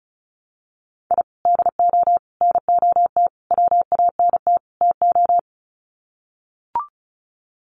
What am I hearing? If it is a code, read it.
Morse code, heard as a keyed tone, about 34 wpm, 700 Hz. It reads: I DO NOT WANT TO
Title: Sets of 5 Words Encore2 35wpm